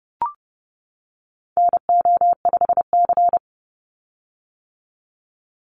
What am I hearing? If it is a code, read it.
Morse code, heard as a keyed tone, about 30 wpm, 700 Hz. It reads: NO5C
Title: USA Call Signs 30wpm